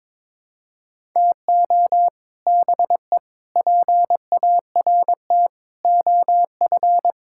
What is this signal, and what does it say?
Morse code, heard as a keyed tone, about 22 wpm, 700 Hz. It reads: TO BE PART OF
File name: practice / 22wpm / Build A Phrase - 5-Word Phrases - 22wpm